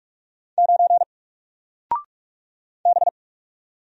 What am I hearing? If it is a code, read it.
Morse code, heard as a keyed tone, about 48 wpm, 700 Hz. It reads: 9 B
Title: Single Letter-Number Alt 45wpm